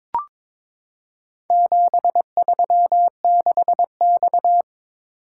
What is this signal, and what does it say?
Morse code, heard as a keyed tone, about 22 wpm, 700 Hz. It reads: 736X
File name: practice / 22wpm / Four Letter-Number Combinations - 22wpm